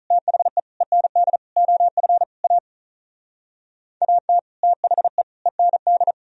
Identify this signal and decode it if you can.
Morse code, heard as a keyed tone, about 46 wpm, 700 Hz. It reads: THE END OF A AT THE END